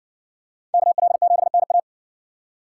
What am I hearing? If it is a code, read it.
Morse code, heard as a keyed tone, about 58 wpm, 700 Hz. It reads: KL7NA